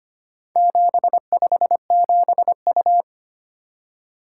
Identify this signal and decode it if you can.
Morse code, heard as a keyed tone, about 25 wpm, 700 Hz. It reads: 757U
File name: practice / 25wpm / Four Letter-Number Combinations - 25wpm